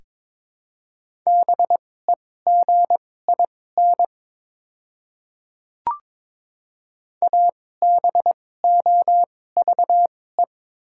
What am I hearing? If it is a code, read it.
Morse code, heard as a keyed tone, about 23 wpm, spaced as for 16 wpm, 700 Hz. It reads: BEGIN ABOVE
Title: 2x Character Spacing - Top 500 Words - 22wpm